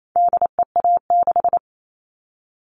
Morse code, heard as a keyed tone, about 28 wpm, 700 Hz. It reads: DEA6
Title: All Cut Numbers - 4-Digit 28wpm